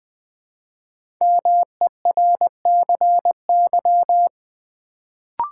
Morse code, heard as a keyed tone, about 20 wpm, 700 Hz. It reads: MERCY
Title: Build A Word - 1 Letter At A Time - 5-Letter Words - 20wpm